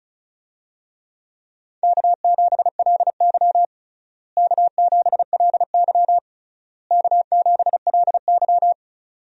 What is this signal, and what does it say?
Morse code, heard as a keyed tone, about 36 wpm, 700 Hz. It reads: K7LY K7LY K7LY
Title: USA Call Signs - 4 Characters - Repeated 3x - 35wpm